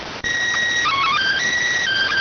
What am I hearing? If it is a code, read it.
SSTV (slow-scan television), PD 290.